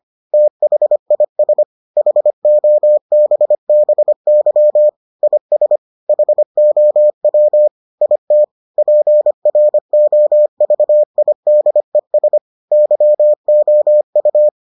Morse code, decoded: THIS HOBBY IS HOW IT PROVIDES YOU

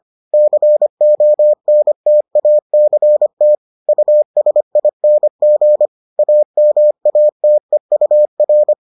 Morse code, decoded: CONTACT USING AMATEUR

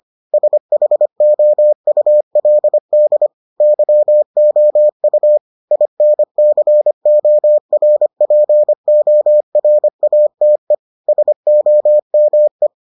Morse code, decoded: SHOULD YOU INCORPORATE SOME

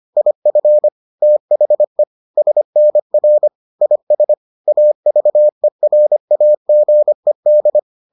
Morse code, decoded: IF THE SNR IS AVERAGED

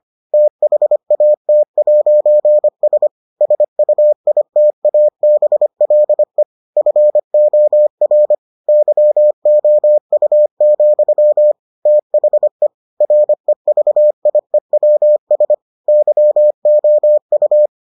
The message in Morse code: THAT'S SUITABLE FOR YOU, THE REVIEWS YOU

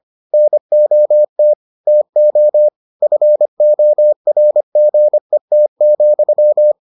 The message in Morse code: NOT TO FORGET,

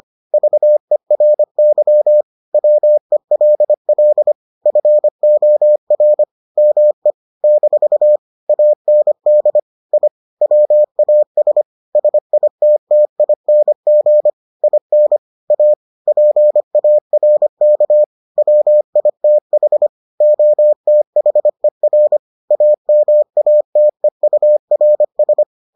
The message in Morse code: VERY WELL FOR ME - AND I WAS SITTING IN A PARK WITH OTHER AMATEURS

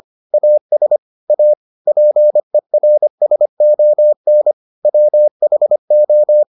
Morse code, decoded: AS A PERSON WHO

